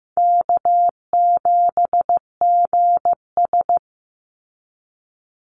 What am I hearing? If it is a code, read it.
Morse code, heard as a keyed tone, about 15 wpm, 700 Hz. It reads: K7GS
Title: Call Signs Repeated 3x - 15wpm